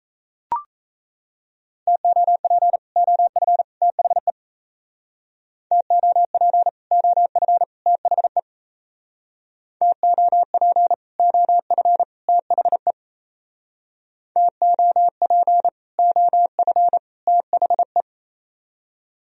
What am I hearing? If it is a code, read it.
Morse code, heard as a keyed tone, about 36 wpm, 700 Hz. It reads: TOP OF THE TOP OF THE TOP OF THE TOP OF THE